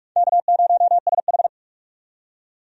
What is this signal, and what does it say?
Morse code, heard as a keyed tone, about 46 wpm, 700 Hz. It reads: K0SH
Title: USA Call Signs - 4 Characters - 45wpm